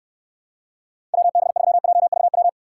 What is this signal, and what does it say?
Morse code, heard as a keyed tone, about 69 wpm, 700 Hz. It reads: WB3JVW